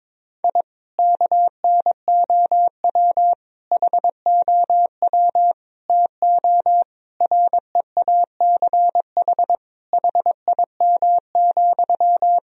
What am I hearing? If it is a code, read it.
Morse code, heard as a keyed tone, about 22 wpm, 700 Hz. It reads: I KNOW HOW TO REACH HIM,
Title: Sentences from Top 1000 Words - Review Part 2 - 22wpm